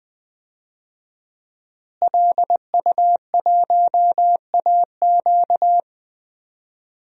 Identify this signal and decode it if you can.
Morse code, heard as a keyed tone, about 20 wpm, 700 Hz. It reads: LU1AQ
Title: DXCC-100 Call Signs - 5 Characters - Repeated 2x - 20wpm